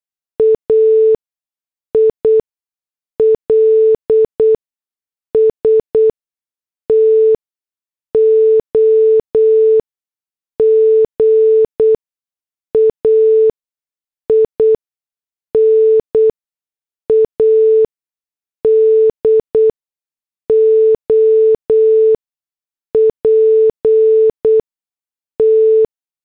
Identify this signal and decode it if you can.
Morse code, heard as a keyed tone, about 8 wpm, spaced as for 6 wpm, 440 Hz. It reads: AILS TO GAIN ADOPT
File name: morse / headlines